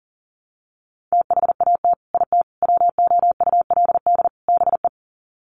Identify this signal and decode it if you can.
Morse code, heard as a keyed tone, about 37 wpm, 700 Hz. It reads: THAT IT WOULD BE
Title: Sets of 4 Words 40wpm